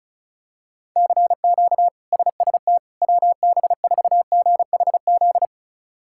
Morse code, heard as a keyed tone, about 34 wpm, 700 Hz. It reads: CQ SST WB4GHZ
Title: K1USN SST QSO Practice - 35wpm